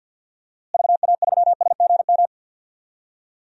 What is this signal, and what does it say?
Morse code, heard as a keyed tone, about 50 wpm, 700 Hz. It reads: VA3SGM
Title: DXCC-100 Call Signs - 6 Characters - Repeated 2x - 50wpm